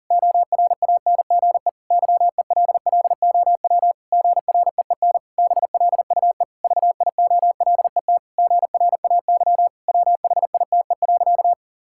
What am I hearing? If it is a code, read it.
Morse code, heard as a keyed tone, about 40 wpm, 700 Hz. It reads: ORANGE YELLOW GREEN BLUE VIOLET GRAY WHITE.
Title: Pangram Challenge v2 - 40wpm